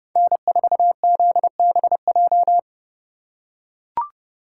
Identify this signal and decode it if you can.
Morse code, heard as a keyed tone, about 31 wpm, 700 Hz. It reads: N4ZBJ